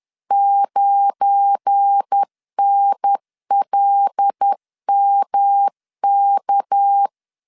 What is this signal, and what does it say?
Morse code, heard as a keyed tone, about 10 wpm, 800 Hz. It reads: 9NLMK
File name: CW / DET / 1 9 1500C S B 10wpm 4439